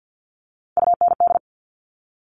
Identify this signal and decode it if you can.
Morse code, heard as a keyed tone, about 50 wpm, 700 Hz. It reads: UND
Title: Top Three Letter Combinations in Words 50wpm